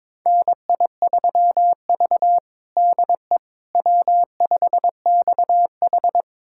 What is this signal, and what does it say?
Morse code, heard as a keyed tone, about 22 wpm, 700 Hz. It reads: NI3V DE W5XH
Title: USA Callsign DE Callsign - 4 Characters - 22wpm